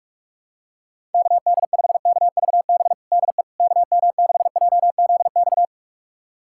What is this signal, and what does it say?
Morse code, heard as a keyed tone, about 46 wpm, 700 Hz. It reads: KD5KVB DE KM6JZX
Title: USA Callsign DE Callsign - 6 Characters - 45wpm